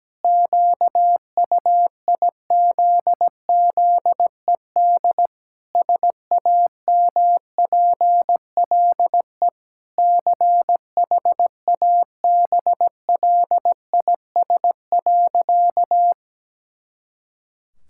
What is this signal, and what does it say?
Morse code, heard as a keyed tone, about 17 wpm, 700 Hz. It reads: QUIZZED SAMPLE CHABLIS.
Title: Pangram Challenge v2 - 17wpm